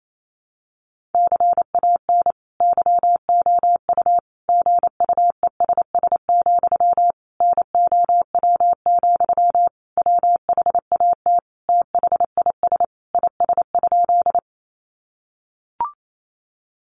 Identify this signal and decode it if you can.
Morse code, heard as a keyed tone, about 28 wpm, 700 Hz. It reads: CAN YOU GUESS, NOW, WHAT THIS IS?
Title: Sentences from Top 1500 Words 28wpm